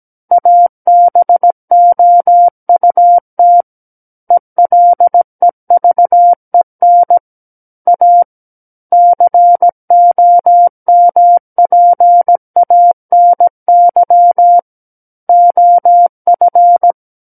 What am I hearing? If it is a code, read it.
Morse code, heard as a keyed tone, about 17 wpm, 700 Hz. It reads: ABOUT ELEVEN A COMPANY OF
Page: SKCC Learning Center - Morse Code Practice Files